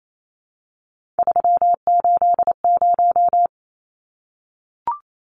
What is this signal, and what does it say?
Morse code, heard as a keyed tone, about 28 wpm, 700 Hz. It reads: 380
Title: Numbers - 3-Digit 28wpm